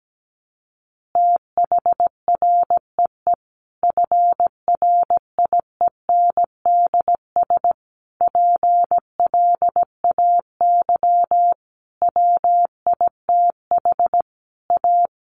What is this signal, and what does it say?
Morse code, heard as a keyed tone, about 17 wpm, 700 Hz. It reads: THREE FRIENDS PLAY WITH A
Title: Easy Phrases 17wpm